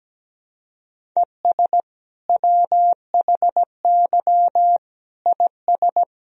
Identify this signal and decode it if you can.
Morse code, heard as a keyed tone, about 17 wpm, 700 Hz. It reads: ES WHY IS
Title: Sentences from Top 200 Words and Abbreviations 17wpm